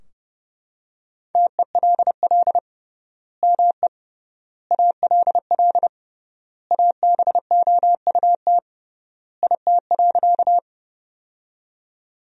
Morse code, heard as a keyed tone, about 30 wpm, 700 Hz. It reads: TELL ME ALL ABOUT IT.
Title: 3x Word-Spacing - Sentences from Top 200 Words - 30wpm